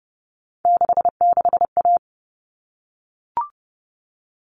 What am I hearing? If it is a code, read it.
Morse code, heard as a keyed tone, about 30 wpm, 700 Hz. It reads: 66A
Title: All Cut Numbers - 3-Digit 30wpm